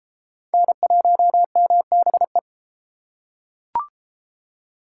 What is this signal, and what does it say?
Morse code, heard as a keyed tone, about 28 wpm, 700 Hz. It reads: N1MBE